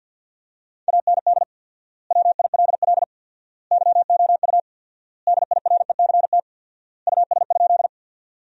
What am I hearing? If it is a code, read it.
Morse code, heard as a keyed tone, about 50 wpm, 700 Hz. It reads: AND WILL YOU DIRECT US?
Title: 4x Word-Spacing - Sentences from Top 400 Words - 50wpm